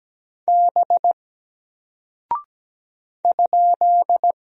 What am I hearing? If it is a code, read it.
Morse code, heard as a keyed tone, about 17 wpm, 700 Hz. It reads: B ?
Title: Single Character (Letter:Number:Punctuation:Prosign) Alt 17wpm